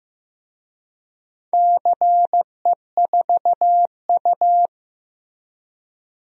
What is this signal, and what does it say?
Morse code, heard as a keyed tone, about 15 wpm, 700 Hz. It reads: CE4U
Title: DXCC-100 Call Signs - 4 Characters - Repeated 2x - 15wpm